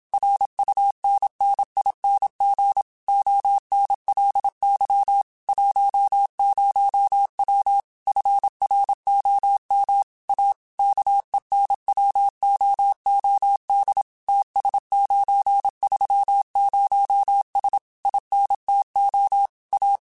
Morse code, heard as a keyed tone, about 27 wpm, 800 Hz. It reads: RUNNING ONLY 10W FROM A KENWOOD TS930S INTO A
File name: QSOs / QSOs-mp3s / WA7QOX 2@25